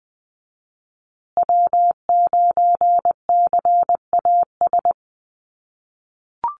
Morse code, heard as a keyed tone, about 20 wpm, 700 Hz. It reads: W9CAS
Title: USA Call Signs Repeated 2x 20wpm